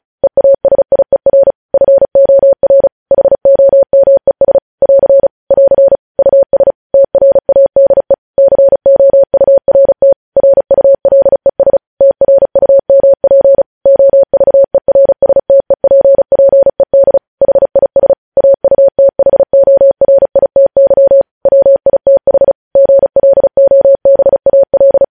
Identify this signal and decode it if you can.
Morse code, heard as a keyed tone, about 36 wpm, 550 Hz. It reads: EASIER FOR HOMES <AR> <AR> US TRADE COURT RULES TRUMP OVERSTEPPED HIS AUTHORITY WITH GLOBAL
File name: podcast / News headlines-1x-35x35